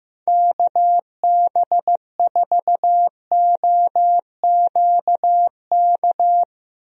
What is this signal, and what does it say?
Morse code, heard as a keyed tone, about 16 wpm, 700 Hz. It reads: KB4OQK
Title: USA Call Signs - 6 Characters - Repeated 3x - 15wpm